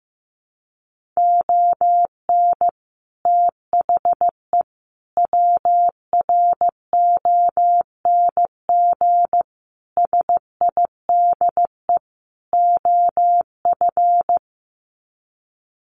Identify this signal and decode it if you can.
Morse code, heard as a keyed tone, about 14 wpm, 700 Hz. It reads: ON THE WRONG SIDE OF
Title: Sets of 5 Words Encore 15wpm